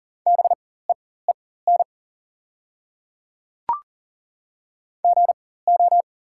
Morse code, heard as a keyed tone, about 40 wpm, spaced as for 19 wpm, 690 Hz. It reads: BEEN GO
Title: 4x Character Spacing - Top 200 Words - 40wpm